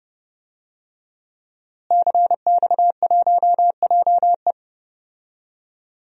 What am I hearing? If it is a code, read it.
Morse code, heard as a keyed tone, about 29 wpm, 700 Hz. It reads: CX1JE
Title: DXCC-100 Call Signs - 5 Characters - Repeated 3x - 30wpm